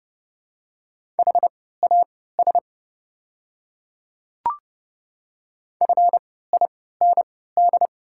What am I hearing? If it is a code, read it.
Morse code, heard as a keyed tone, about 30 wpm, spaced as for 17 wpm, 700 Hz. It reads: HAS FIND
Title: 3x Character Spacing - Top 100 Words - 30wpm